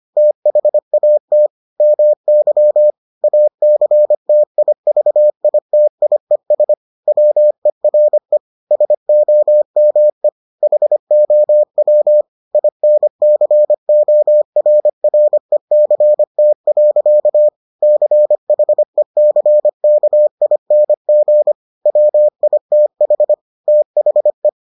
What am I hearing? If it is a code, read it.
Morse code, heard as a keyed tone, about 25 wpm, 600 Hz. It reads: THAT MY ACTIVITIES WERE SOMEHOW INCORRECT. CHECKING WITH THE